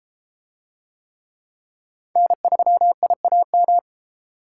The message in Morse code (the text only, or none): N3IAM